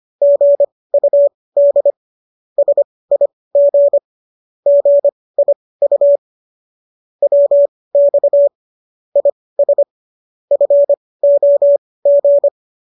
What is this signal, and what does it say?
Morse code, heard as a keyed tone, about 25 wpm, spaced as for 18 wpm, 570 Hz. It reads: GUD SIG GIU WX IS FOG